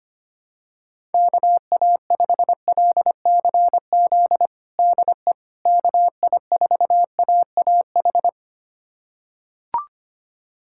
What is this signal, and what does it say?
Morse code, heard as a keyed tone, about 25 wpm, 700 Hz. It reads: KA5LCZ DE KI4AAH
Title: USA Callsign DE Callsign - 6 Characters - 25wpm